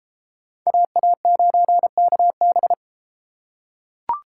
Morse code, heard as a keyed tone, about 32 wpm, 700 Hz. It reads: AA9KB